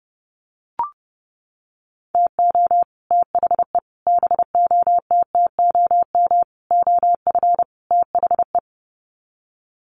Morse code, heard as a keyed tone, about 30 wpm, 700 Hz. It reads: TO THE BOTTOM OF THE
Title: Sets of 5 Words 30wpm